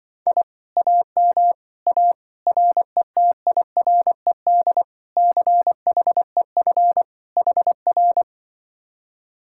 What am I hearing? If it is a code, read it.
Morse code, heard as a keyed tone, about 24 wpm, 700 Hz. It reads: I AM A RETIRED CHEF HR
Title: Speed-Racing Common QSO Phrases Encore - 20wpm